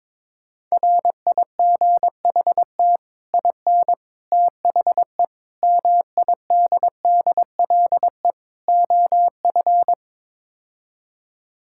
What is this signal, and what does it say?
Morse code, heard as a keyed tone, about 22 wpm, 700 Hz. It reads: RIGHT IN THE MIDDLE OF
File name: practice / 22wpm / Build A Phrase - 5-Word Phrases - 22wpm